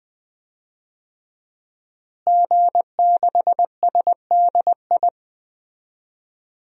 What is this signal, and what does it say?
Morse code, heard as a keyed tone, about 20 wpm, 700 Hz. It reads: G6SDI
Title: DXCC-100 Call Signs - 5 Characters - Repeated 2x - 20wpm